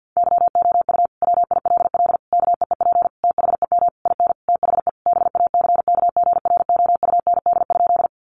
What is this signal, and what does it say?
Morse code, heard as a keyed tone, about 50 wpm, 700 Hz. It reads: YOU WILL KEEP THEM IN THE BACKGROUND?
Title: Sentences from Top 1300 Words 50wpm